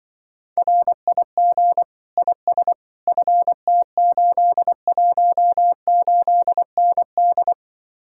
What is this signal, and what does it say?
Morse code, heard as a keyed tone, about 25 wpm, 700 Hz. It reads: RIG IS FT818ND